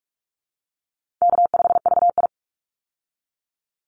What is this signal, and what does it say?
Morse code, heard as a keyed tone, about 44 wpm, 700 Hz. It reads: K5VI
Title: Easy Call Signs Repeated 3x - 45wpm